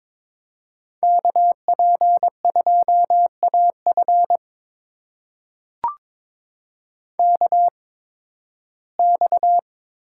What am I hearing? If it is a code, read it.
Morse code, heard as a keyed tone, about 22 wpm, spaced as for 14 wpm, 700 Hz. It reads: KP2AF KX
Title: USA Call Signs - 5 Characters - 8x Character Spacing - 22wpm